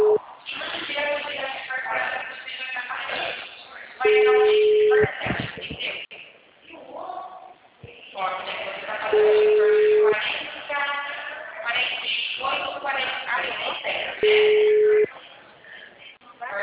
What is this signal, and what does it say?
Ringback tone (Europe, 425 Hz), with speech.